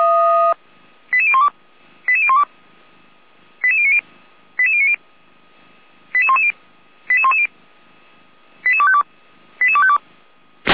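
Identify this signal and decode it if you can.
Five-tone selective call (ZVEI), heard as quick runs of five tones, tones of about 70 ms. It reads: 88012 88012 88090 88090 88110 88110 88252 88252
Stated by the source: Received on SDR as ZVEI five-tone selective call; archived by feuerwehr-orlamuende.org (fire brigade Orlamünde).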